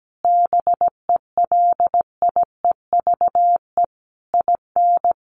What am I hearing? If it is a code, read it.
Morse code, heard as a keyed tone, about 17 wpm, 700 Hz. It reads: BELIEVE IN
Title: Sentences from Top 1500 Words 17wpm